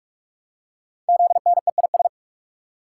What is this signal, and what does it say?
Morse code, heard as a keyed tone, about 44 wpm, 700 Hz. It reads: ZNEIS
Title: Code Groups - 5 Letters - 45wpm